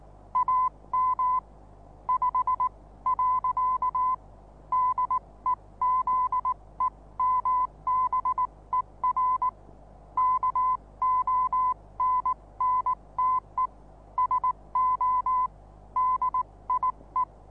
Morse code, heard as a keyed tone, about 19 wpm, spaced as for 16 wpm, 1010 Hz. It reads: AM 5. DEZEMBER KONNTE SO DIE
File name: repos / Morsekurs / DJ4UF Morsen26